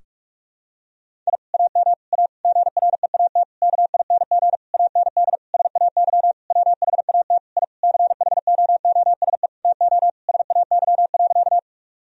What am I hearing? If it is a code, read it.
Morse code, heard as a keyed tone, about 46 wpm, 700 Hz. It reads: I AM A GREAT KING AND SAY WHAT I CHOOSE TO SAY.
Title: Sentences from Top 1000 Words - Review Part 4 - 45wpm